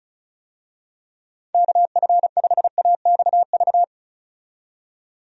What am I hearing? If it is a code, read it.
Morse code, heard as a keyed tone, about 34 wpm, 700 Hz. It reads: KF5AXV
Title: USA Call Signs - 6 Characters - 35wpm